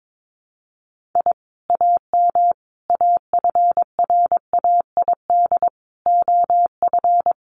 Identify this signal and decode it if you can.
Morse code, heard as a keyed tone, about 22 wpm, 700 Hz. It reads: I AM AFRAID OF
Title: Sentences from Top 1500 Words 22wpm